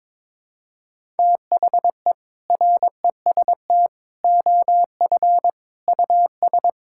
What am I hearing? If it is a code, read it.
Morse code, heard as a keyed tone, about 22 wpm, 700 Hz. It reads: THE REST OF US
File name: practice / 22wpm / Sentences with 4-Word Phrases - 22wpm